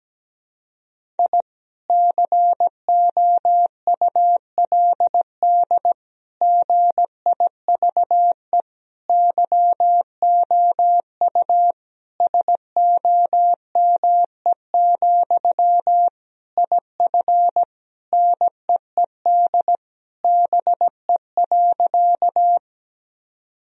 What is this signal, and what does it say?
Morse code, heard as a keyed tone, about 17 wpm, 700 Hz. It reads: I COULD GIVE YOU SOME, IF NEED BE.
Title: Sentences from Top 1000 Words - Review Part 1 - 17wpm